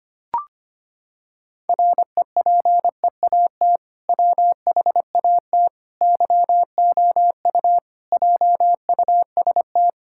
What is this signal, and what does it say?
Morse code, heard as a keyed tone, about 25 wpm, 700 Hz. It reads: REPEAT WHAT YOU JUST
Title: Sentences from Top 1600 Words 25wpm